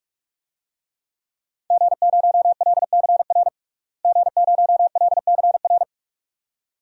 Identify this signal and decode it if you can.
Morse code, heard as a keyed tone, about 46 wpm, 700 Hz. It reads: G0LCR G0LCR
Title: DXCC-100 Call Signs - 5 Characters - Repeated 2x - 45wpm